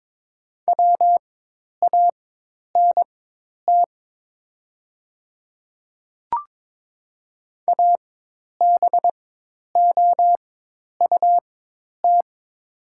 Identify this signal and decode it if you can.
Morse code, heard as a keyed tone, about 22 wpm, spaced as for 10 wpm, 700 Hz. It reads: WANT ABOUT
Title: 4x Character Spacing - Top 200 Words - 22wpm